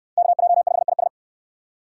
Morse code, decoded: KJ5IS

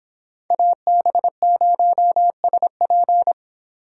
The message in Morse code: AB0SP

